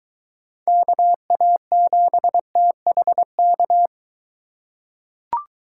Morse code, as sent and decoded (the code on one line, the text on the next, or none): -.- .- --... - .... -.-
KA7THK